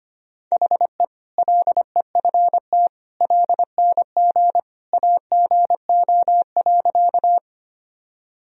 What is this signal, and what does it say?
Morse code, heard as a keyed tone, about 25 wpm, 700 Hz. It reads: HE LEFT LNG AGO.
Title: Sentences from Top 500 Words and Abbreviations 25wpm